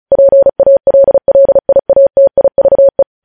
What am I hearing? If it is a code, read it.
Morse code, heard as a keyed tone, about 36 wpm, 550 Hz. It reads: PALLIATIVE
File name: podcast / News headlines-1x-35x35